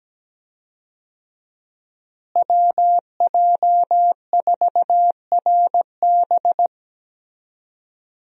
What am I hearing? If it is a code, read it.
Morse code, heard as a keyed tone, about 17 wpm, 700 Hz. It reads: WJ4RB
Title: USA Call Signs - 5 Characters - Repeated 3x - 17wpm